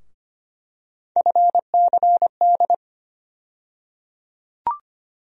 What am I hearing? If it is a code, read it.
Morse code, heard as a keyed tone, about 25 wpm, 700 Hz. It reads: FCD
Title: USA Call Sign Postfixes - 3 Letters 25wpm